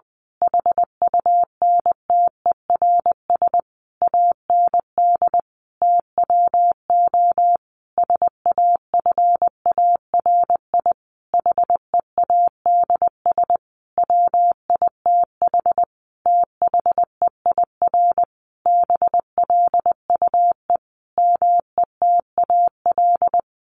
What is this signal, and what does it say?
Morse code, heard as a keyed tone, about 20 wpm, 700 Hz. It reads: HUNTERS AND TWO SAFARI HEADS WITH THEIR BLUE METAL
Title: A Sound Of Thunder 20wpm